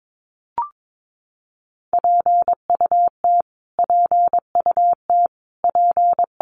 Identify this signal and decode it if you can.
Morse code, heard as a keyed tone, about 22 wpm, 700 Hz. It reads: PUT PUT P
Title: Top 500 Words Repeated 22wpm